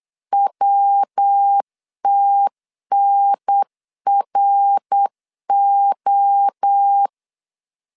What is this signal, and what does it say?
Morse code, heard as a keyed tone, about 8 wpm, 800 Hz. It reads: WTNRO
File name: CW / DET / 1 3 1500C S B 8wpm 4141